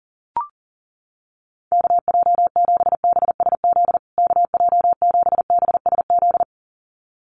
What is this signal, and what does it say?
Morse code, heard as a keyed tone, about 40 wpm, 700 Hz. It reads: KJ7BSZ KJ7BSZ
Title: USA Call Signs Repeated 2x 40wpm